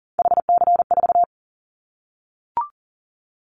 Morse code, heard as a keyed tone, about 42 wpm, 700 Hz. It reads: HC4
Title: Callsign Prefixes for Top 100 DXCC - 40wpm